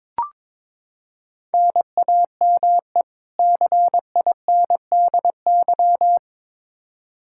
Morse code, heard as a keyed tone, about 22 wpm, 700 Hz. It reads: NAME CINDY
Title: K1USN SST QSO Practice 22wpm